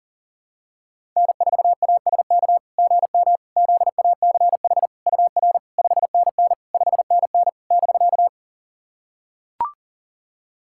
Morse code, heard as a keyed tone, about 40 wpm, 690 Hz. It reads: N4ASK GM ZACH UR 5NN 5NN <BK>
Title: SOTA QSO Practice 40wpm